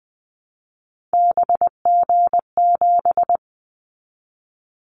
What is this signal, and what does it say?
Morse code, heard as a keyed tone, about 20 wpm, 700 Hz. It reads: BG7
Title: Callsign Prefixes for Top 100 DXCC - 20wpm